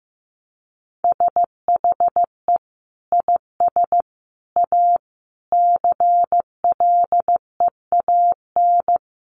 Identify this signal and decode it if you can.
Morse code, heard as a keyed tone, about 15 wpm, 700 Hz. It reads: SHE IS A CLEAN
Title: Sentences from Top 1100 Words 15wpm